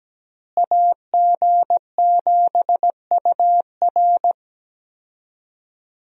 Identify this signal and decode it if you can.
Morse code, heard as a keyed tone, about 17 wpm, 700 Hz. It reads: AG7UR
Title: USA Call Signs - 5 Characters - Repeated 2x - 17wpm